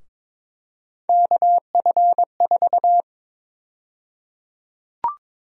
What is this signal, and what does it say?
Morse code, heard as a keyed tone, about 22 wpm, 700 Hz. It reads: KF4
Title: USA Call Sign Prefixes - 22wpm